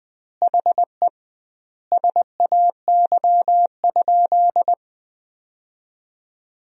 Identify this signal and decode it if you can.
Morse code, heard as a keyed tone, about 20 wpm, 700 Hz. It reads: HE SAY?
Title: 2x Word-Spacing - Sentences from Top 200 Words - 20wpm